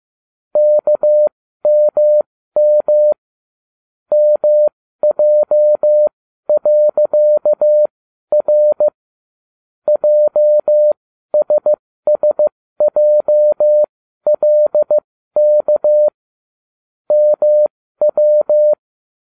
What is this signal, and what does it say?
Morse code, heard as a keyed tone, about 15 wpm, spaced as for 12 wpm, 600 Hz. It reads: KMM MJ.R JSSJLK MW